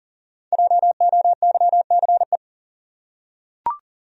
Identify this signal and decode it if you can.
Morse code, heard as a keyed tone, about 40 wpm, 700 Hz. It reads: JOYCE
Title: USA Common Names - Max 5 Letters - 40wpm